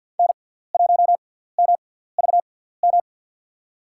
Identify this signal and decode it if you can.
Morse code, heard as a keyed tone, about 47 wpm, spaced as for 17 wpm, 700 Hz. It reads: N1MVM
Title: USA Call Signs - 6x Character Spacing - 50wpm